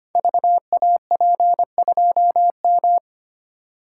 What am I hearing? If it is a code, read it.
Morse code, heard as a keyed tone, about 25 wpm, 700 Hz. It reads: VAP2M
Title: Code Groups - 5 Letter:Numbers - 25wpm